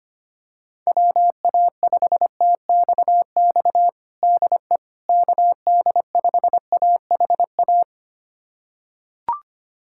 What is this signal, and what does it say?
Morse code, heard as a keyed tone, about 25 wpm, 700 Hz. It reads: WA5TXX DE KD5AHA
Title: USA Callsign DE Callsign - 6 Characters - 25wpm